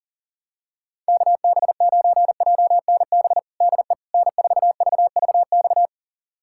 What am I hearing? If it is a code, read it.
Morse code, heard as a keyed tone, about 40 wpm, 700 Hz. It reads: KB9JNB DE N4VVX
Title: USA Callsign DE Callsign 40wpm